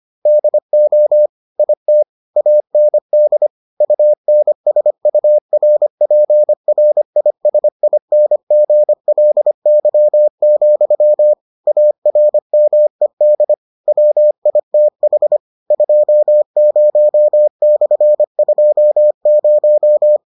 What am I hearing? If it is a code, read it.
Morse code, heard as a keyed tone, about 25 wpm, 600 Hz. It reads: DO IT AND UNSURPRISINGLY, ARMED WITH 20/20